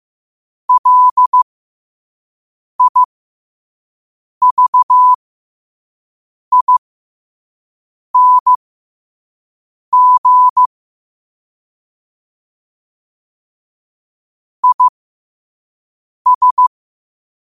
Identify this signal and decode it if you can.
Morse code, heard as a keyed tone, about 15 wpm, spaced as for 5 wpm, 1000 Hz. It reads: LIVING IS
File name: qotdfiles / 5wpm